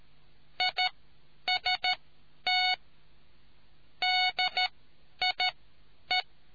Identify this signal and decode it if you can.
Morse code, heard as a keyed tone, about 13 wpm, spaced as for 9 wpm, 740 Hz. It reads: IST DIE